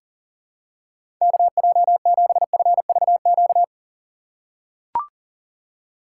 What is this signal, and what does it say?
Morse code, heard as a keyed tone, about 37 wpm, 700 Hz. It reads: KJ7FVQ
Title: USA Call Signs - 6x Character Spacing - 40wpm